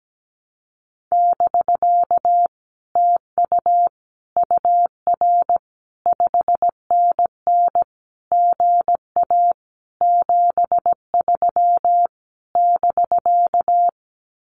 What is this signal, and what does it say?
Morse code, heard as a keyed tone, about 17 wpm, 700 Hz. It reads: <BK> TU UR 5NN GA 73 <BK>
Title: Common QSO Phrases Encore 17wpm